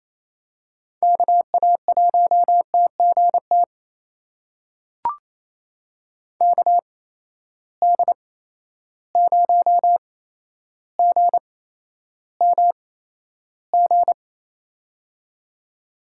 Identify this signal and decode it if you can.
Morse code, heard as a keyed tone, about 28 wpm, spaced as for 12 wpm, 700 Hz. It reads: KA1TGT KD0GMG
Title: USA Call Signs - 6 Characters - 8x Character Spacing - 28wpm